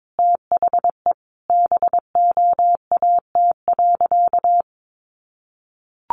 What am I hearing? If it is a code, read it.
Morse code, heard as a keyed tone, about 22 wpm, 700 Hz. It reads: THE BOAT.
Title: Sentences from Top 1400 Words 22wpm